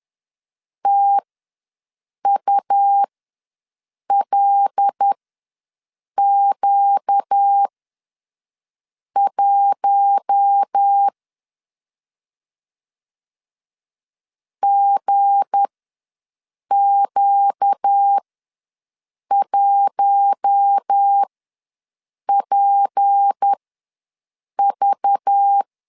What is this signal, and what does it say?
Morse code, heard as a keyed tone, about 10 wpm, spaced as for 6 wpm, 800 Hz. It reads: TULQ1 GQ1PV
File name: LEZIONI / 7 GF9Z / 7 G ws 10wpm 3843